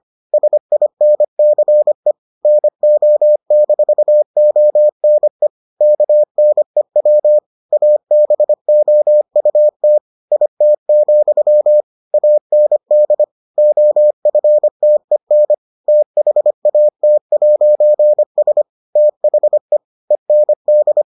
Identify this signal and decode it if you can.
Morse code, heard as a keyed tone, about 25 wpm, 600 Hz. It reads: SINCE NO-ONE KNEW ABOUT IT, AND OFTEN THAT'S THE END